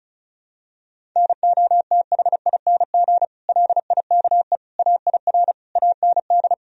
Morse code, heard as a keyed tone, about 34 wpm, 700 Hz. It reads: NOTHING LIKE AIR AND
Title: Sentences from Top 1000 Words - Review Part 4 - 35wpm